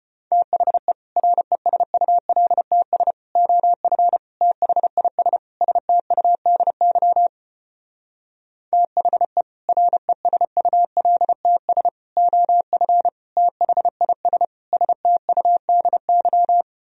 Morse code, heard as a keyed tone, about 33 wpm, 700 Hz. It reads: THE RESULTS OF THIS STUDY THE RESULTS OF THIS STUDY